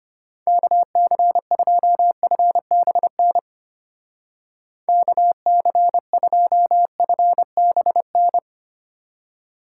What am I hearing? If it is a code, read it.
Morse code, heard as a keyed tone, about 30 wpm, 700 Hz. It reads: KC2FBN KC2FBN
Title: Speed Racing - USA 2x3 Call Signs 25wpm